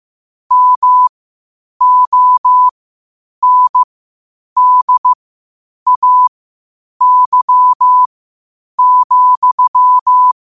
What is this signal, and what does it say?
Morse code, heard as a keyed tone, about 15 wpm, spaced as for 8 wpm, 1000 Hz. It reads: MONDAY,
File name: qotdfiles / 8wpm